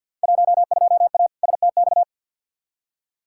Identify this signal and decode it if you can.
Morse code, heard as a keyed tone, about 52 wpm, 700 Hz. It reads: 12A STX
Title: Field Day Exchanges Encore - 50wpm